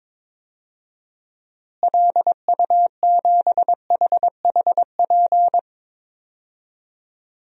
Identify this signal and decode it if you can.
Morse code, heard as a keyed tone, about 21 wpm, 700 Hz. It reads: LU7HHP
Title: DXCC-100 Call Signs - 6 Characters - Repeated 2x - 22wpm